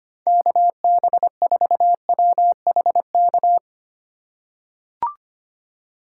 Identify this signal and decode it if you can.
Morse code, heard as a keyed tone, about 25 wpm, 700 Hz. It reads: KB4WHK
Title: USA Call Signs - 6 Characters - 3x Character Spacing - 25wpm